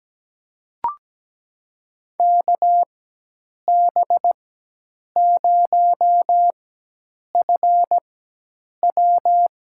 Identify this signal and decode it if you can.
Morse code, heard as a keyed tone, about 17 wpm, spaced as for 8 wpm, 700 Hz. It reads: KB0FW
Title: USA Call Signs - 4x Character Spacing - 17wpm